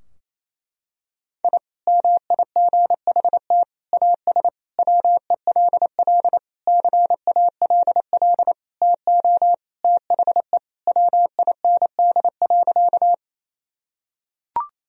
Morse code, heard as a keyed tone, about 28 wpm, 700 Hz. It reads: I MIGHT AS WELL CALL TO THE WIND.
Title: Sentences from Top 1000 Words - Review Part 2 - 28wpm